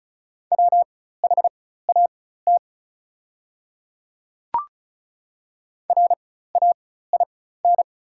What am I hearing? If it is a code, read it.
Morse code, heard as a keyed tone, about 36 wpm, spaced as for 16 wpm, 700 Hz. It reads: WHAT RAIN
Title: 4x Character Spacing - Top 300 Words - 35wpm